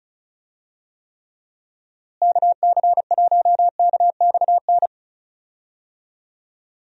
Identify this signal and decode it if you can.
Morse code, heard as a keyed tone, about 36 wpm, 700 Hz. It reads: KC1KXN
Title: USA Call Signs - 6 Characters - Repeated 2x - 35wpm